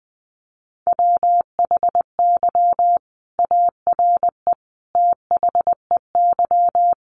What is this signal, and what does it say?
Morse code, heard as a keyed tone, about 20 wpm, 700 Hz. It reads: WHY ARE THEY
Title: Sentences from Top 200 Words 20wpm